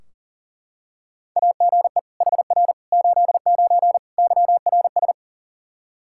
Morse code, heard as a keyed tone, about 40 wpm, 700 Hz. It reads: AGE HR 89 YRS
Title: Practice QSO - Exchange 3 - More Info - 40wpm